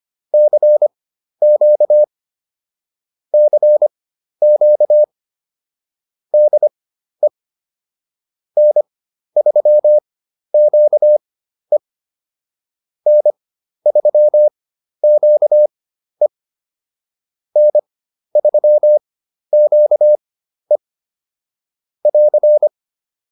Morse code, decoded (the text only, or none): CQ CQ DE N3QE N3QE N3QE <AR>